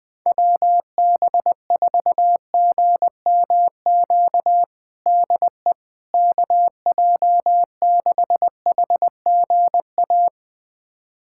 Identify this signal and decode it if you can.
Morse code, heard as a keyed tone, about 21 wpm, 700 Hz. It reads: WB4GMQ DE KJ6HGA
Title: USA Callsign DE Callsign - 6 Characters - 20wpm